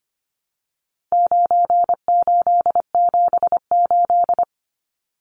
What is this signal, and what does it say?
Morse code, heard as a keyed tone, about 25 wpm, 700 Hz. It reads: 9878
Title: Numbers - 4-Digit 25wpm